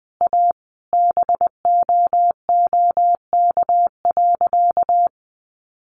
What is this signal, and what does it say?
Morse code, heard as a keyed tone, about 20 wpm, 700 Hz. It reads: A BOOK.
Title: Additional Easy Phrases 20wpm